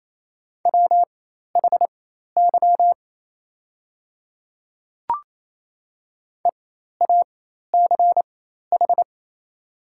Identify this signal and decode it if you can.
Morse code, heard as a keyed tone, about 28 wpm, spaced as for 13 wpm, 690 Hz. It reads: WHY EACH
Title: 4x Character Spacing - Top 300 Words - 28wpm